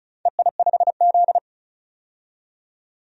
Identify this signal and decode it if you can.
Morse code, heard as a keyed tone, about 36 wpm, 700 Hz. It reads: EI5Z
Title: DXCC-100 Call Signs - 4 Characters - 35wpm